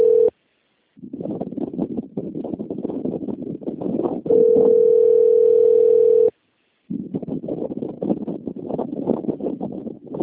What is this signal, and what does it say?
Ringback tone (North America), 440 + 480 Hz.